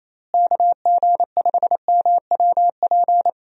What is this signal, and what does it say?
Morse code, heard as a keyed tone, about 27 wpm, 700 Hz. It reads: KG5MWP
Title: USA Call Signs - 6 Characters - Repeated 3x - 28wpm